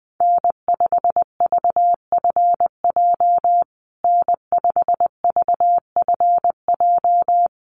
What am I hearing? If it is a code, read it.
Morse code, heard as a keyed tone, about 20 wpm, 700 Hz. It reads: N5VFJ N5VFJ
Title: USA Call Signs Repeated 3x 20wpm